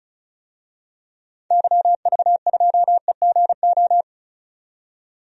Morse code, heard as a keyed tone, about 36 wpm, 700 Hz. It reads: YV2EGO
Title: DXCC-100 Call Signs - 6 Characters - 35wpm